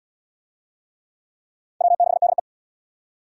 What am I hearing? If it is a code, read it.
Morse code, heard as a keyed tone, about 67 wpm, 700 Hz. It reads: W6DE